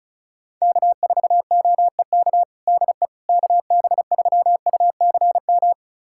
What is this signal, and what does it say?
Morse code, heard as a keyed tone, about 36 wpm, 690 Hz. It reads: K4OEK DE KB3UCM
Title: USA Callsign DE Callsign 35wpm